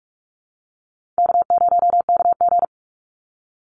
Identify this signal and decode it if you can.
Morse code, heard as a keyed tone, about 44 wpm, 700 Hz. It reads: K0KG
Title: Easy Call Signs Repeated 2x - 45wpm